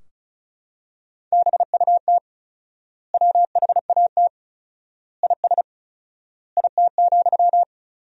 Morse code, heard as a keyed tone, about 36 wpm, 700 Hz. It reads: BUT WHAT IS IT,
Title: 4x Word-Spacing - Sentences from Top 300 Words - 35wpm